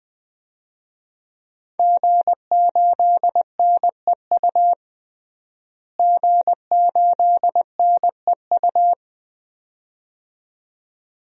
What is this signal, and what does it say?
Morse code, heard as a keyed tone, about 20 wpm, 700 Hz. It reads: G8NEU G8NEU
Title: DXCC-100 Call Signs - 5 Characters - Repeated 2x - 20wpm